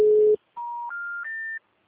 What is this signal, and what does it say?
Special information tone.
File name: out- -2008- - - .58583